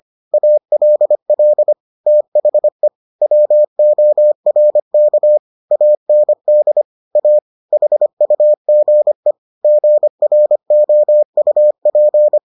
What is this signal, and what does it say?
Morse code, heard as a keyed tone, about 25 wpm, 600 Hz. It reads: ALL THE WORK AND A HUGE GROUP